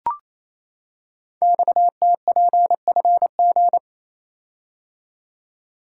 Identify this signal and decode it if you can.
Morse code, heard as a keyed tone, about 28 wpm, 700 Hz. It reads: XTPFG
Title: Code Groups - 5 Letters - 28wpm